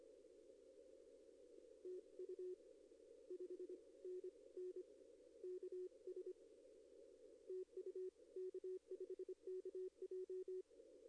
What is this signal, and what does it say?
Morse code, heard as a keyed tone, about 24 wpm, 370 Hz. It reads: TU 5NN KS TU K5KJ
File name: contests / arrl-160-k5tr-op-w5tn -1300